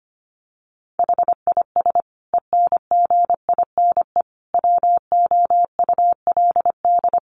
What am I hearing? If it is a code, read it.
Morse code, heard as a keyed tone, about 25 wpm, 700 Hz. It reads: HIS ENGINE WOULD
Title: Sentences from Top 1500 Words 25wpm